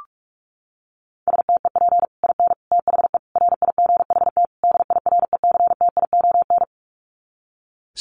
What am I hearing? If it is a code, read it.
Morse code, heard as a keyed tone, about 44 wpm, 700 Hz. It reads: STEP IN THE RIGHT DIRECTION
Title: Sets of 5 Words Encore 45wpm